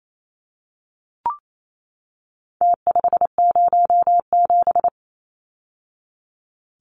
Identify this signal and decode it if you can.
Morse code, heard as a keyed tone, about 28 wpm, 700 Hz. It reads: T507
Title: Numbers - Contest Style - 4-Digit 28wpm